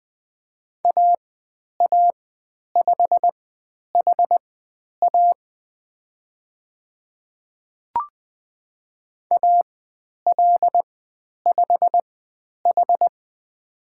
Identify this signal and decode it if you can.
Morse code, heard as a keyed tone, about 20 wpm, spaced as for 10 wpm, 700 Hz. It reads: AA5HA AL5H
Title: 095 Lesson 16g - Letter c - Call Signs 20wpm